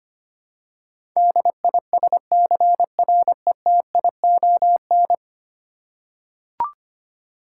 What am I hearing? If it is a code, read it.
Morse code, heard as a keyed tone, about 25 wpm, 700 Hz. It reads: DISCRETION
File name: Ten Letter Words - 25wpm